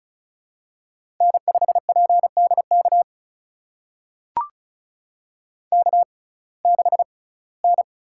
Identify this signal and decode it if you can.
Morse code, heard as a keyed tone, about 36 wpm, spaced as for 22 wpm, 700 Hz. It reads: N5PDK K6N
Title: USA Call Signs - 5 Characters - 6x Character Spacing - 35wpm